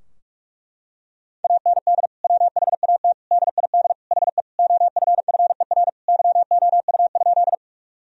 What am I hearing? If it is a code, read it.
Morse code, heard as a keyed tone, about 46 wpm, 700 Hz. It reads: AND WHAT DID HE OFFER YOU?
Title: Sentences from Top 1000 Words - Review Part 4 - 45wpm